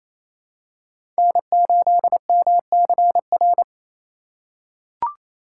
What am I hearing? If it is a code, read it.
Morse code, heard as a keyed tone, about 27 wpm, 700 Hz. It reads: N8MCR
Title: USA Call Signs - 4x Character Spacing - 28wpm